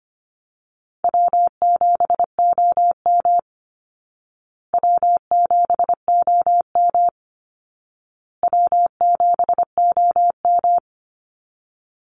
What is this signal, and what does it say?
Morse code, heard as a keyed tone, about 25 wpm, 700 Hz. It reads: W7OM W7OM W7OM
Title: Easy Call Signs Repeated 3x - 25wpm